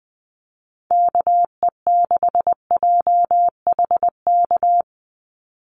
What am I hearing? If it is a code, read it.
Morse code, heard as a keyed tone, about 20 wpm, 700 Hz. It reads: KE6JHK
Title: USA Call Signs 20wpm